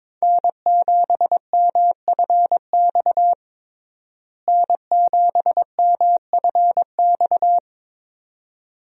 Morse code, heard as a keyed tone, about 22 wpm, 700 Hz. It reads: N7MFX N7MFX
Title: USA Call Signs - 5 Characters - Repeated 2x - 22wpm